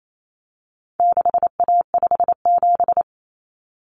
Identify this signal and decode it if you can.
Morse code, heard as a keyed tone, about 28 wpm, 700 Hz. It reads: 6A57
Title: Common Cut Numbers - 4-Digit 28wpm